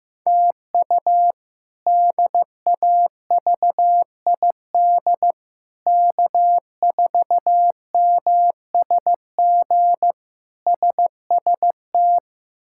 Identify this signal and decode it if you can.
Morse code, heard as a keyed tone, about 15 wpm, 700 Hz. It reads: TU DAVID K4MSG SST
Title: K1USN SST QSO Practice - 15wpm